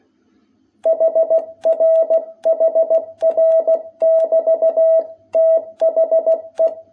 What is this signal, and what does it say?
Morse code, heard as a keyed tone, about 16 wpm, 660 Hz. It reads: HR HR<BT>THE
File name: jh8cbh / 2021cw